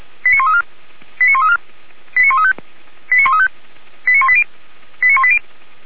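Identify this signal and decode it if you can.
Five-tone selective call (ZVEI), heard as quick runs of five tones, tones of about 70 ms. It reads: 89134 89134 89135 89135 89180 89180
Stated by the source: ZVEI five-tone selective call 89132 received by SDR on channel path 464, logged by the Feuerwehr Orlamünde audio server.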